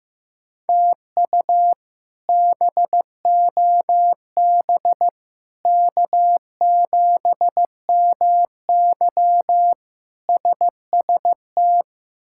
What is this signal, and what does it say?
Morse code, heard as a keyed tone, about 15 wpm, 700 Hz. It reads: TU BOB K7MY SST
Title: K1USN SST QSO Practice - 15wpm